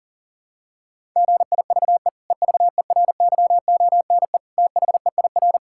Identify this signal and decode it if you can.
Morse code, heard as a keyed tone, about 40 wpm, 700 Hz. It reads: GIVE EVERYONE THEIR
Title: Sentences from Top 1000 Words - Review Part 6 - 40wpm